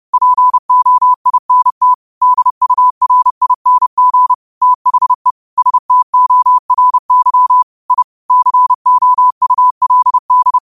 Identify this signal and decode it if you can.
Morse code, heard as a keyed tone, about 30 wpm, 1000 Hz. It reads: POINT DURING THE STORY I COULD